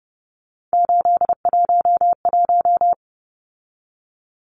Morse code, heard as a keyed tone, about 30 wpm, 700 Hz. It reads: 811
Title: Numbers - 3-Digit 30wpm